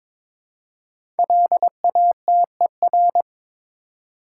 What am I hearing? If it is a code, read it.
Morse code, heard as a keyed tone, about 22 wpm, 700 Hz. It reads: LATER
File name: 5 Letter Words - First 3 Letters Given - 22wpm